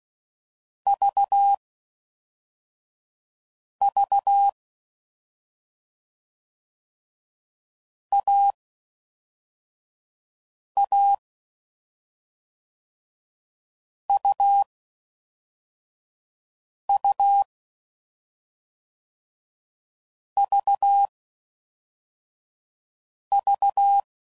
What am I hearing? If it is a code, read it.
Morse code, heard as a keyed tone, about 16 wpm, 800 Hz. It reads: V V A A U U V V
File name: Combinations / Difficult Combos / AUV Short